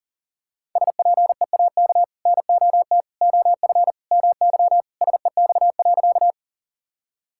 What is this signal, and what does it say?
Morse code, heard as a keyed tone, about 40 wpm, 700 Hz. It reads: SPEAK NOT OF MY SEX.
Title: Sentences from Top 1600 Words 40wpm